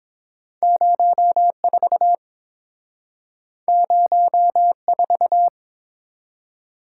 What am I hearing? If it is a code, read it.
Morse code, heard as a keyed tone, about 30 wpm, 700 Hz. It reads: 04 04